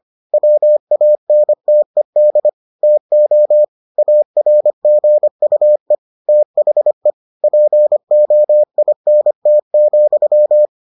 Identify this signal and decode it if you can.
Morse code, heard as a keyed tone, about 25 wpm, 600 Hz. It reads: WANTED TO ARGUE THE POINT,